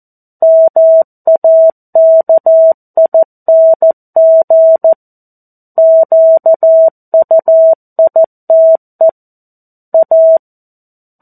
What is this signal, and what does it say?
Morse code, heard as a keyed tone, about 14 wpm, 650 Hz. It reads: MAKING QUITE A
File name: 02-Chapter-02 14 WPM